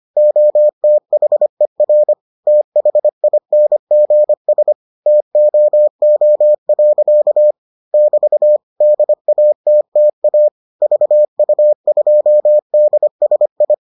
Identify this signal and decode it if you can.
Morse code, heard as a keyed tone, about 25 wpm, 600 Hz. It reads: OTHER THINGS TOO. <BT> DATTA VU2DSI